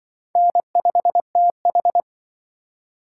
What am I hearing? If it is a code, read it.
Morse code, heard as a keyed tone, about 23 wpm, 700 Hz. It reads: N5TH
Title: Speed Racing - USA Call Signs 20wpm